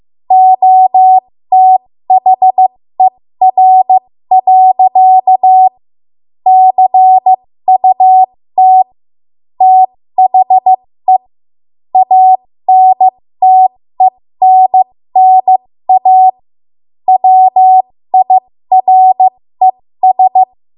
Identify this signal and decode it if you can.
Morse code, heard as a keyed tone, about 15 wpm, spaced as for 13 wpm, 750 Hz. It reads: OTHER. CUT THE ANTENNA WIRES
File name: Morse / 13WPM